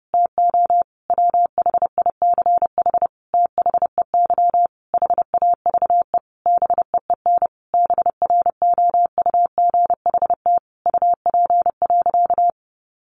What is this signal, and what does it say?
Morse code, heard as a keyed tone, about 30 wpm, 700 Hz. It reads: TO WHICH THEY HAVE BEEN BROUGHT UP.
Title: Sentences from Top 1500 Words 30wpm